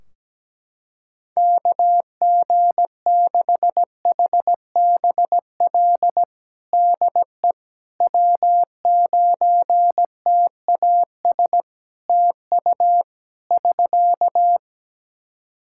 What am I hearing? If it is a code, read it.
Morse code, heard as a keyed tone, about 17 wpm, 700 Hz. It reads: KG6HBL DE W9TAS TU <SK>
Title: Practice QSO - Exchange 4 - Ending - 17wpm